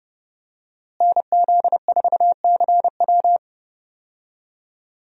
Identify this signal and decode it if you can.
Morse code, heard as a keyed tone, about 30 wpm, 700 Hz. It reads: NZ4CW
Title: USA Call Signs - 5 Characters - Repeated 2x - 30wpm